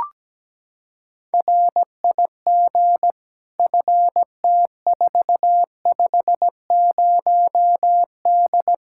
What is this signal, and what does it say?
Morse code, heard as a keyed tone, about 17 wpm, 700 Hz. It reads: RIG FT450D
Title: Practice QSO - Exchange 2 - Info - 17wpm